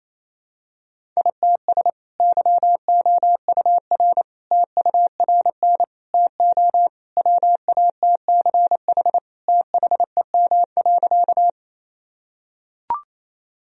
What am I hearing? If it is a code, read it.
Morse code, heard as a keyed tone, about 28 wpm, 700 Hz. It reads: ITS YOUR TURN TO WATCH THEM.
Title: Sentences from Top 1000 Words - Review Part 1 - 28wpm